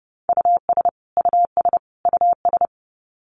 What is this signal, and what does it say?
Morse code, heard as a keyed tone, about 30 wpm, 700 Hz. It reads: US US US
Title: Top 500 Words Repeated 30wpm